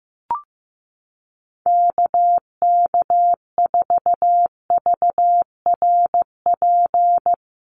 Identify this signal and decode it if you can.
Morse code, heard as a keyed tone, about 15 wpm, 700 Hz. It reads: KK4VRP
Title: USA Call Signs Repeated 3x 15wpm